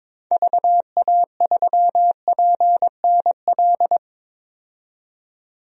Morse code, heard as a keyed tone, about 22 wpm, 700 Hz. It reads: VA3PNL
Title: DXCC-100 Call Signs - 6 Characters - 22wpm